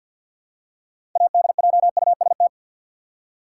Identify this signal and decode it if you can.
Morse code, heard as a keyed tone, about 52 wpm, 700 Hz. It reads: ADJUST
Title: Six Letter Words - 50wpm